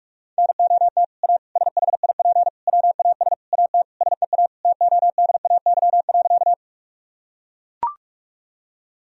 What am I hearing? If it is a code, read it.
Morse code, heard as a keyed tone, about 42 wpm, 700 Hz. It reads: NOT A SHIP WAS AT SEA TODAY.
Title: Sentences from Top 1000 Words - Review Part 2 - 45wpm